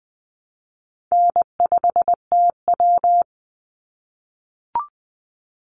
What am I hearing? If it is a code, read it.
Morse code, heard as a keyed tone, about 20 wpm, 700 Hz. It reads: N5TW
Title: Easy Call Signs - 20wpm